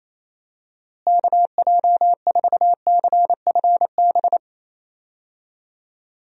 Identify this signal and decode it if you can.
Morse code, heard as a keyed tone, about 27 wpm, 700 Hz. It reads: KJ4CFB
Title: USA Call Signs - 6 Characters - Repeated 2x - 28wpm